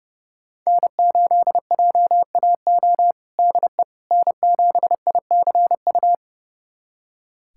Morse code, heard as a keyed tone, about 29 wpm, 700 Hz. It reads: N8JAO DE N7ICU
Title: USA Callsign DE Callsign 30wpm